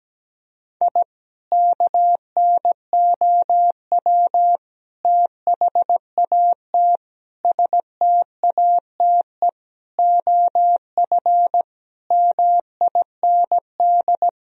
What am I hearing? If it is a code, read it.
Morse code, heard as a keyed tone, about 17 wpm, 700 Hz. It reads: I KNOW THAT STATE OF MIND
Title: Sentences from Top 1000 Words - Review Part 2 - 17wpm